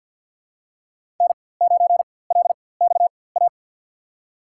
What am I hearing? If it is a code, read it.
Morse code, heard as a keyed tone, about 52 wpm, spaced as for 23 wpm, 700 Hz. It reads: N9LXA
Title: USA Call Signs - 5 Characters - 4x Character Spacing - 50wpm